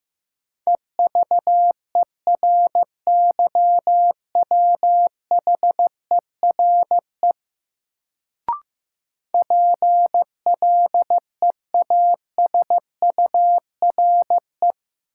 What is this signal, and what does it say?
Morse code, heard as a keyed tone, about 15 wpm, 700 Hz. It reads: EVERYWHERE PLEASURE
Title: Sentences from Top 1600 Words - New Word Drill - 15wpm